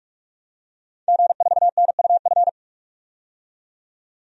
Morse code, heard as a keyed tone, about 46 wpm, 700 Hz. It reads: G4NUF
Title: DXCC-100 Call Signs - 5 Characters - 45wpm